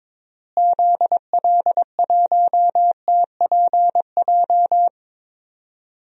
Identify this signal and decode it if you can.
Morse code, heard as a keyed tone, about 22 wpm, 700 Hz. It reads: ZL1TPJ
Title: DXCC-100 Call Signs - 6 Characters - Repeated 3x - 22wpm